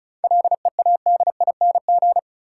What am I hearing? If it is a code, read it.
Morse code, heard as a keyed tone, about 34 wpm, 700 Hz. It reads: LEADING
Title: Seven Letter Words - 35wpm